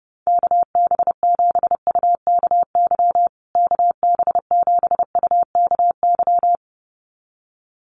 Morse code, heard as a keyed tone, about 30 wpm, 700 Hz. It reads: KB7UKY KB7UKY
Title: USA Call Signs Repeated 2x 30wpm